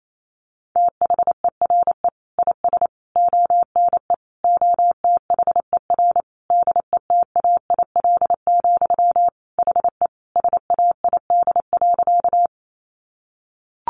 Morse code, heard as a keyed tone, about 28 wpm, 700 Hz. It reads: THERE IS ONE OTHER DETAIL, HE SAID.
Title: Sentences from Top 1500 Words 28wpm